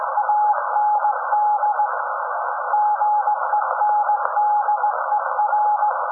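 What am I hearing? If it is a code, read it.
Morse code, heard as a keyed tone, about 18 wpm, 840 Hz, with noise. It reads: AND GENDER
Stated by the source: SAQ Grimeton Radio, Alexanderson 200 kW alternator on 17.2 kHz, received in Potsdam with a very strong signal on World Radio Day.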